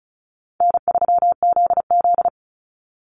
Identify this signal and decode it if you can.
Morse code, heard as a keyed tone, about 34 wpm, 700 Hz. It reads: N3ZZ
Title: Easy Call Signs Repeated 2x - 35wpm